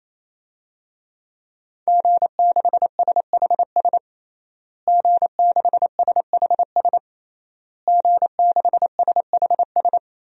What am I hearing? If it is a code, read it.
Morse code, heard as a keyed tone, about 27 wpm, 700 Hz. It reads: G6SHS G6SHS G6SHS
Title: DXCC-100 Call Signs - 5 Characters - Repeated 3x - 28wpm